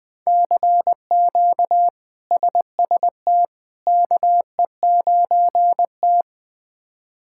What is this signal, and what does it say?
Morse code, heard as a keyed tone, about 20 wpm, 700 Hz. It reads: CQ SST KE9T
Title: K1USN SST QSO Practice - 20wpm